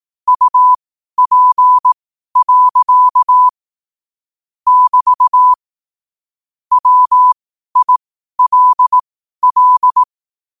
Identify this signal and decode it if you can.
Morse code, heard as a keyed tone, about 18 wpm, spaced as for 12 wpm, 1000 Hz. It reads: UP. <BT> WILL
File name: qotdfiles / 12wpm